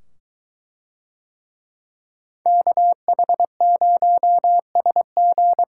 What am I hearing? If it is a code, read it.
Morse code, heard as a keyed tone, about 22 wpm, 700 Hz. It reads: KH0SG